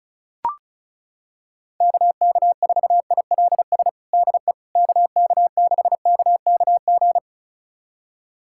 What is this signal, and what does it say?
Morse code, heard as a keyed tone, about 34 wpm, 700 Hz. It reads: KK4ILS DE KK6KKG
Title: USA Callsign DE Callsign - 6 Characters - 35wpm